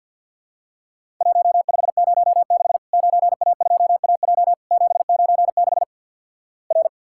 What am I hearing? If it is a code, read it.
Morse code, heard as a keyed tone, about 50 wpm, 700 Hz. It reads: 1506 9A2AJ 796 R
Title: Worked All Europe DX Contest - QTC Exchange with Repeats - 50wpm